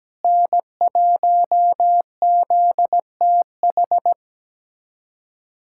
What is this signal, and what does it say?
Morse code, heard as a keyed tone, about 17 wpm, 700 Hz. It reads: N1ZTH
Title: USA Callsign DE Callsign - 5 Characters - 17wpm